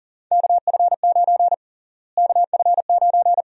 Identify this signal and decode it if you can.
Morse code, heard as a keyed tone, about 40 wpm, 700 Hz. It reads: KF9 KF9
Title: USA Call Sign Prefixes - Repeated 2x - 40wpm